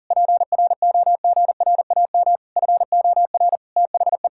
Morse code, heard as a keyed tone, about 40 wpm, 700 Hz. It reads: PROGRAM FOR THE
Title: Sentences from Top 1000 Words - Review Part 5 - 40wpm